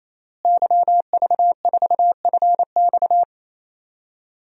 Morse code, heard as a keyed tone, about 27 wpm, 700 Hz. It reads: YV4FX
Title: DXCC-100 Call Signs - 5 Characters - Repeated 3x - 28wpm